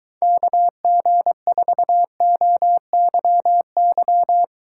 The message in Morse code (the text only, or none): KG4OYY